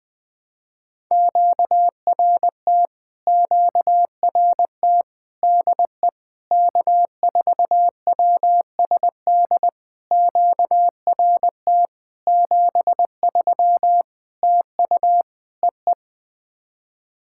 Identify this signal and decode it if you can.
Morse code, heard as a keyed tone, about 20 wpm, 700 Hz. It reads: QRT QRT DE K4WSD QRT 73 TU EE
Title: POTA QSO Practice - 20wpm